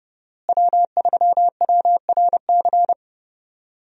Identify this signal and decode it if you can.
Morse code, heard as a keyed tone, about 30 wpm, 700 Hz. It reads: W3WRC
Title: USA Call Signs - 5 Characters - Repeated 2x - 30wpm